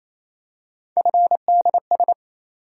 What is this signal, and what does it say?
Morse code, heard as a keyed tone, about 27 wpm, 700 Hz. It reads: FDS
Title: USA Call Sign Postfixes - 3 Letters 28wpm